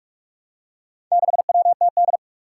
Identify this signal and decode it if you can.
Morse code, heard as a keyed tone, about 44 wpm, 700 Hz. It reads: 6WTD
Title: Four Challenging Letter-Number Combinations 45wpm